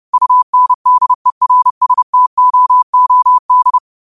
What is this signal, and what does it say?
Morse code, heard as a keyed tone, about 30 wpm, 1000 Hz. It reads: UNDERSTOOD
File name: qotdfiles / 30wpm